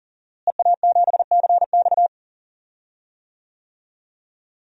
Morse code, heard as a keyed tone, about 38 wpm, 700 Hz. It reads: EA7CX
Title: DXCC-100 Call Signs - 5 Characters - Repeated 2x - 40wpm